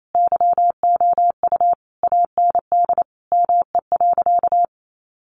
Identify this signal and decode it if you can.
Morse code, heard as a keyed tone, about 28 wpm, 700 Hz. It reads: YOU AND ME.
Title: Sentences from Top 1200 Words 28wpm